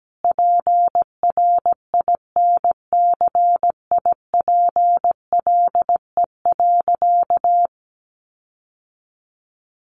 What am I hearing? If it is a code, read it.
Morse code, heard as a keyed tone, about 17 wpm, 700 Hz. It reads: PRINCIPLE.
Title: Sentences from Top 1400 Words 17wpm